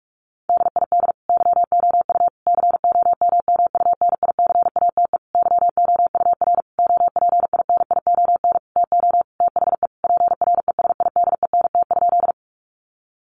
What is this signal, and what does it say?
Morse code, heard as a keyed tone, about 46 wpm, 700 Hz. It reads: DID YOU COMMUNICATE YOUR OPINION TO THE PRESIDENT?
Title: Sentences from Top 1300 Words 45wpm